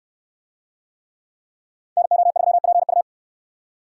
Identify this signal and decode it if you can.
Morse code, heard as a keyed tone, about 62 wpm, 700 Hz. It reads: NO3PU